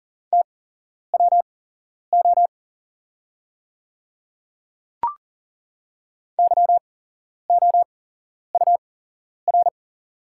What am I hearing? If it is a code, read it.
Morse code, heard as a keyed tone, about 40 wpm, spaced as for 11 wpm, 700 Hz. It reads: TWO YOUR
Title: 8x Character Spacing - Top 100 Words - 40wpm